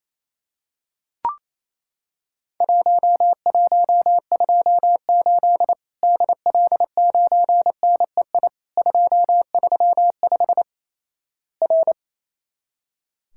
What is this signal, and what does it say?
Morse code, heard as a keyed tone, about 27 wpm, 700 Hz. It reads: 1128 DL9NEI 235 R
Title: Worked All Europe DX Contest - QTC Exchange - 28wpm